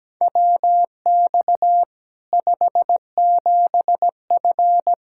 Morse code, heard as a keyed tone, about 17 wpm, 700 Hz. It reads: WX 57F
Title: K1USN SST QSO Practice 17wpm